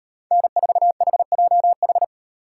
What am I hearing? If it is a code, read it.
Morse code, heard as a keyed tone, about 38 wpm, 700 Hz. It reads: N4HJH